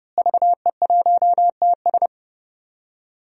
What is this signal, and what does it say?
Morse code, heard as a keyed tone, about 30 wpm, 700 Hz. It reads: VE1TS
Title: DXCC-100 Call Signs - 5 Characters - Repeated 3x - 30wpm